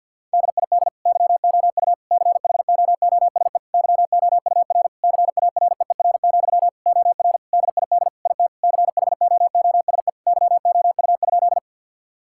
Morse code, heard as a keyed tone, about 50 wpm, 700 Hz. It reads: DID YOU CHOOSE YOUR CAREER, OR DID IT CHOOSE YOU?
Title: Sentences from Top 1000 Words - Review Part 6 - 50wpm